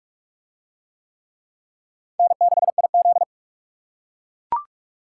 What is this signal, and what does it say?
Morse code, heard as a keyed tone, about 44 wpm, 700 Hz. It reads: N6IZ